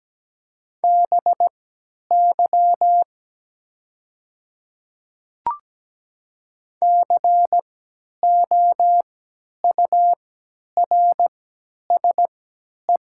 Morse code, heard as a keyed tone, about 17 wpm, spaced as for 10 wpm, 700 Hz. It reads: BY COURSE
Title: 3x Character Spacing - Top 500 Words - 17wpm